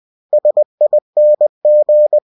Morse code, heard as a keyed tone, about 20 wpm, 600 Hz. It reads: SING
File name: cw-books / winnie-the-pooh / winnie-20wpm-page-0075